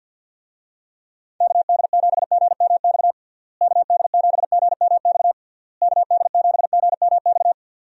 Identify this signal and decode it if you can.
Morse code, heard as a keyed tone, about 46 wpm, 700 Hz. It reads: KD7GMX KD7GMX KD7GMX
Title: USA Call Signs - 6 Characters - Repeated 3x - 50wpm